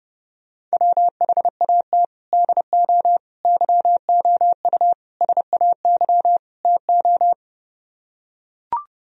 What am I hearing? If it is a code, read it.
Morse code, heard as a keyed tone, about 29 wpm, 700 Hz. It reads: WHAT DO YOU SAY TO